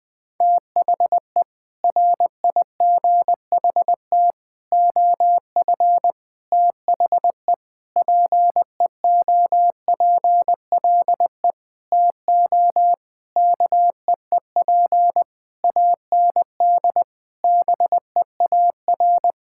Morse code, heard as a keyed tone, about 21 wpm, 700 Hz. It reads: THE RIGHT OF THE PEOPLE TO KEEP AND BEAR
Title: Sentences with 9-Word Phrases - 20wpm